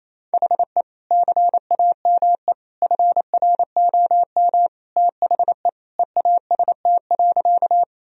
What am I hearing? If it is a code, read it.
Morse code, heard as a keyed tone, about 28 wpm, 690 Hz. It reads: HE CAME FROM THE EAST.
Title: Sentences from Top 1000 Words - Review Part 3 - 28wpm